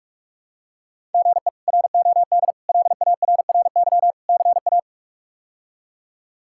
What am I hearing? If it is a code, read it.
Morse code, heard as a keyed tone, about 44 wpm, 700 Hz. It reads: GE ROD LARRY CA
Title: K1USN SST QSO Practice - 45wpm